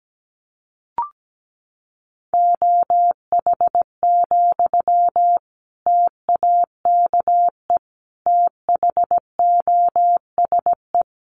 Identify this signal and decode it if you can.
Morse code, heard as a keyed tone, about 17 wpm, 700 Hz. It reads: OH, TAKE THOSE
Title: Sentences from Top 500 Words 17wpm